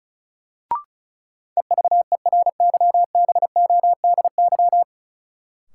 Morse code, heard as a keyed tone, about 36 wpm, 700 Hz. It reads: EVERYBODY
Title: Sentences from Top 1400 Words - New Word Drill - 35wpm